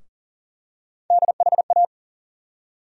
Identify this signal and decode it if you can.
Morse code, heard as a keyed tone, about 40 wpm, 700 Hz. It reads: DHA
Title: USA Call Sign Postfixes - 3 Letters 40wpm